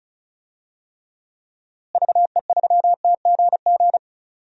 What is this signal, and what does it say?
Morse code, heard as a keyed tone, about 36 wpm, 700 Hz. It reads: VE3TGG
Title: DXCC-100 Call Signs - 6 Characters - Repeated 3x - 35wpm